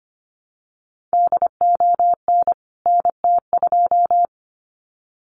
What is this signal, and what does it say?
Morse code, heard as a keyed tone, about 25 wpm, 700 Hz. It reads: DON NT2
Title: CWT Exchange 25wpm